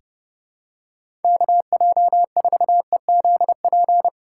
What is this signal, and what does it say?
Morse code, heard as a keyed tone, about 32 wpm, 700 Hz. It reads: KJ4EZP
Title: USA Call Signs - 6 Characters - 3x Character Spacing - 30wpm